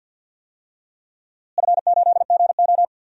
Morse code, heard as a keyed tone, about 53 wpm, 700 Hz. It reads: F8GO
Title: DXCC-100 Call Signs - 4 Characters - Repeated 3x - 50wpm